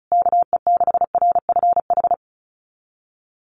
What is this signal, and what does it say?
Morse code, heard as a keyed tone, about 34 wpm, 700 Hz. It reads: KE6RFH
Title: USA Call Signs Repeated 3x 35wpm